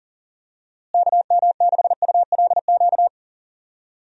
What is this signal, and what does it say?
Morse code, heard as a keyed tone, about 39 wpm, 700 Hz. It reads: KM6ULQ
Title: USA Call Signs - 6 Characters - Repeated 3x - 40wpm